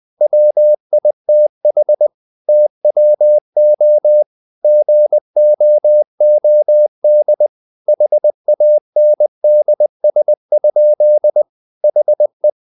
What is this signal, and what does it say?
Morse code, heard as a keyed tone, about 20 wpm, 600 Hz. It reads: WITH TWO GOOD HANDS? HE
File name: the old man and the sea part 2 20wpm